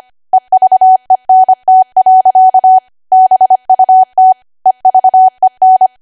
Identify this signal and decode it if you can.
Morse code, heard as a keyed tone, about 25 wpm, 750 Hz. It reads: EVENT. BUT EVEN
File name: Archive / 25 WPM / 25WPM